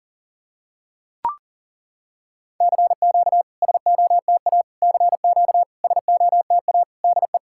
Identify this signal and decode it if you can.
Morse code, heard as a keyed tone, about 40 wpm, 690 Hz. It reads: CQ SOTA CQ SOTA DE
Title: SOTA QSO Practice 40wpm